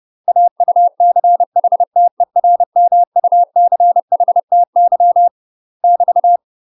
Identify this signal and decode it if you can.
Morse code, heard as a keyed tone, about 30 wpm, 700 Hz. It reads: AUCHTERMUCHTY <BT>
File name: UK MASTER QSOs TRAINING / 30WPM / UK MASTER QSOs 0032